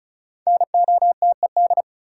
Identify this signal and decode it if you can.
Morse code, heard as a keyed tone, about 36 wpm, 700 Hz. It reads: NOTED
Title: Sets of 5 Words - New Word Drill - 35wpm